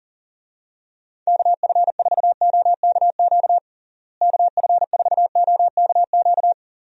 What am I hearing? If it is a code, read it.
Morse code, heard as a keyed tone, about 37 wpm, 700 Hz. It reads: KF4OKQ KF4OKQ
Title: USA Call Signs - 6 Characters - Repeated 3x - 40wpm